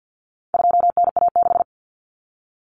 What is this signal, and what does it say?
Morse code, heard as a keyed tone, about 50 wpm, 700 Hz. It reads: 2NA6
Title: Common Cut Numbers - 4-Digit 50wpm